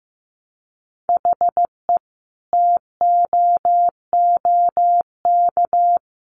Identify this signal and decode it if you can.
Morse code, heard as a keyed tone, about 15 wpm, 700 Hz. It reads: HE TOOK
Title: Sentences from Top 1500 Words 15wpm